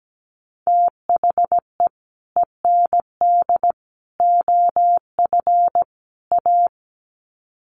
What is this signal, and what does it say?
Morse code, heard as a keyed tone, about 17 wpm, 700 Hz. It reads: THE END OF A
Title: Sets of 4 Words 17wpm